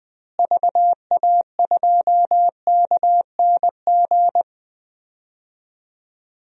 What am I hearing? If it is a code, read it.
Morse code, heard as a keyed tone, about 20 wpm, 700 Hz. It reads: VA2KNG
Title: DXCC-100 Call Signs - 6 Characters - Repeated 2x - 20wpm